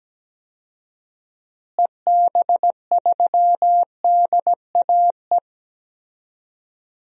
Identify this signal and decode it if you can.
Morse code, heard as a keyed tone, about 17 wpm, 700 Hz. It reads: EB3DAE
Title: DXCC-100 Call Signs - 6 Characters - Repeated 3x - 17wpm